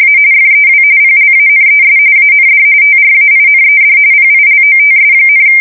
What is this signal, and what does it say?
RTTY, about 45.45 baud. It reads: A FEW PLANES AND TRAINED PILOTS,
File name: fsk170 / USS-New-Jersey